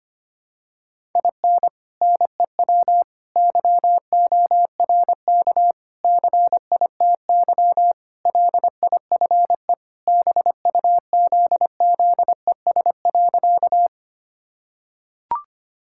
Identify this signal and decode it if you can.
Morse code, heard as a keyed tone, about 25 wpm, 700 Hz. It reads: IN NEW YORK CITY LIFE BUZZES.
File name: practice / 25wpm / Sentences with 4-Word Phrases - 25wpm